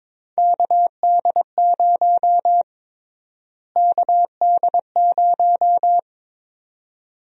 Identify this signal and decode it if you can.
Morse code, heard as a keyed tone, about 22 wpm, 700 Hz. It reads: KD0 KD0
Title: USA Call Sign Prefixes - Repeated 2x - 22wpm